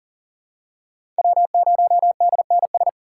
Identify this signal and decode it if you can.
Morse code, heard as a keyed tone, about 40 wpm, 700 Hz. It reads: W0DNS
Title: USA Call Signs - 5 Characters - 3x Character Spacing - 40wpm